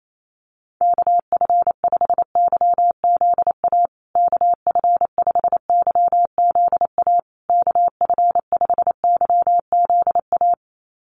Morse code, heard as a keyed tone, about 28 wpm, 700 Hz. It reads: KF5YZA KF5YZA KF5YZA
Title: USA Call Signs Repeated 3x 28wpm